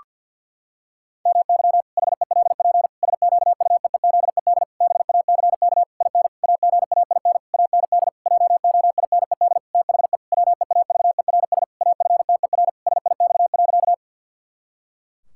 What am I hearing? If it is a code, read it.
Morse code, heard as a keyed tone, about 50 wpm, 700 Hz. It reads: MY HELP SQUEEZED BACK IN AGAIN AND JOINED THE WEAVERS AFTER SIX.
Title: Pangram Challenge v2 - 50wpm